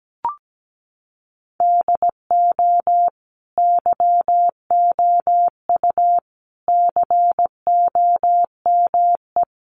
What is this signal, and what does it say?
Morse code, heard as a keyed tone, about 17 wpm, 700 Hz. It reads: DO YOU COME
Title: Sentences from Top 1400 Words 17wpm